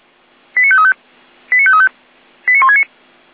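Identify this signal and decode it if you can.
Five-tone selective call (ZVEI), heard as quick runs of five tones, tones of about 70 ms. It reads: 89535 89535 89169
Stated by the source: ZVEI five-tone selective call received on an SDR.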